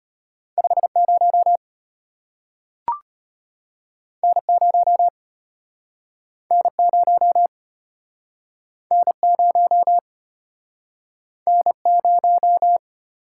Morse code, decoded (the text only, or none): 50 N0 N0 N0 N0